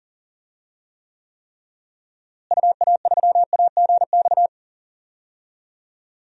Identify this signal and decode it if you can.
Morse code, heard as a keyed tone, about 40 wpm, 700 Hz. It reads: UA3AGX
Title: DXCC-100 Call Signs - 6 Characters - Repeated 3x - 40wpm